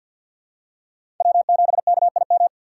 Morse code, heard as a keyed tone, about 52 wpm, 700 Hz. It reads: W7KIM
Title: USA Call Signs - 2x Character Spacing - 50wpm